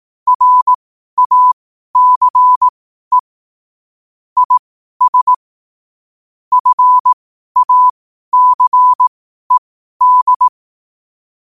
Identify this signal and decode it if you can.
Morse code, heard as a keyed tone, about 18 wpm, spaced as for 12 wpm, 1000 Hz. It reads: RACE IS FACED
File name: qotdfiles / 12wpm